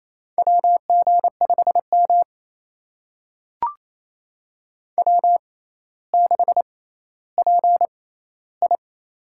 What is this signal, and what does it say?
Morse code, heard as a keyed tone, about 28 wpm, spaced as for 14 wpm, 700 Hz. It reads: WG5M W6PI
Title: USA Call Signs - 4 Characters - 6x Character Spacing - 28wpm